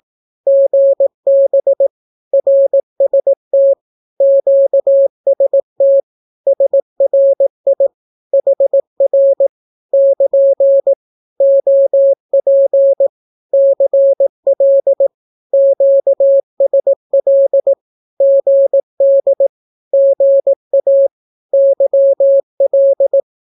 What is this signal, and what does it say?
Morse code, heard as a keyed tone, about 18 wpm, 550 Hz. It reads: GB RST QST SRI HR <KN> OP CL QSL GD GA YL
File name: naukapodstaw / skroty / skroty